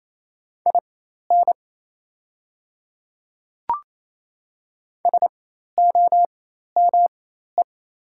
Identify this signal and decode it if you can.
Morse code, heard as a keyed tone, about 28 wpm, spaced as for 13 wpm, 700 Hz. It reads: IN SOME
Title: 4x Character Spacing - Top 100 Words - 28wpm